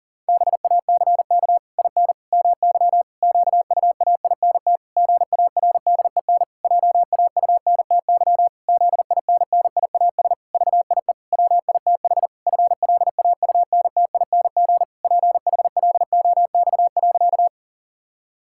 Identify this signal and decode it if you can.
Morse code, heard as a keyed tone, about 40 wpm, 700 Hz. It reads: BACK IN MY QUAINT GARDEN JAUNTY ZINNIAS VIE WITH FLAUNTING PHLOX.
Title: Pangram Challenge v2 - 40wpm